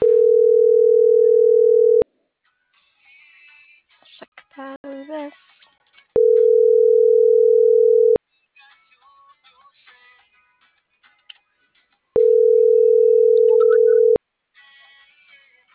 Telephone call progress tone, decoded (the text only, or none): ringback tone